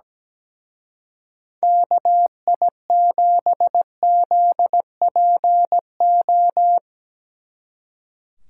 Morse code, heard as a keyed tone, about 19 wpm, 700 Hz. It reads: KI7ZPO
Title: Speed Racing - USA Call Signs 17wpm